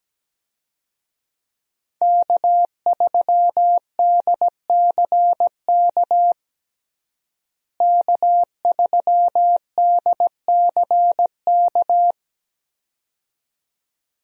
Morse code, heard as a keyed tone, about 17 wpm, 700 Hz. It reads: K3DCK K3DCK
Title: USA Call Signs - 5 Characters - Repeated 2x - 17wpm